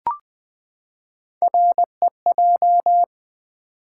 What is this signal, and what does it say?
Morse code, heard as a keyed tone, about 20 wpm, 700 Hz. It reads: REJ
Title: Build A Word - 3 Letters At A Time - 9-Letter Words - 20wpm